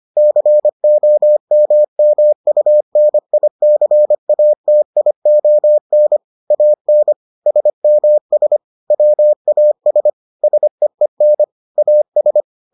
Morse code, heard as a keyed tone, about 25 wpm, 600 Hz. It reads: COMMUNICATION AN SMS WAS SEEN AS